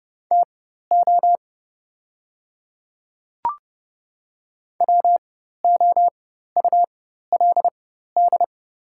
Morse code, heard as a keyed tone, about 30 wpm, spaced as for 14 wpm, 700 Hz. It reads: TO WOULD
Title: 4x Character Spacing - Top 100 Words - 30wpm